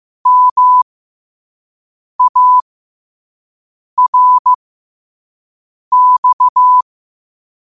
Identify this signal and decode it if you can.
Morse code, heard as a keyed tone, about 15 wpm, spaced as for 5 wpm, 1000 Hz. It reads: MARX
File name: qotdfiles / 5wpm